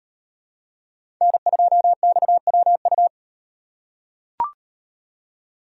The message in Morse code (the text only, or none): N2XWU